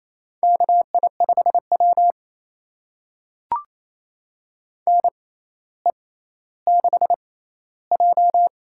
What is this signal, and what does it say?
Morse code, heard as a keyed tone, about 28 wpm, spaced as for 14 wpm, 700 Hz. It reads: KI5W NE6J
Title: USA Call Signs - 4 Characters - 6x Character Spacing - 28wpm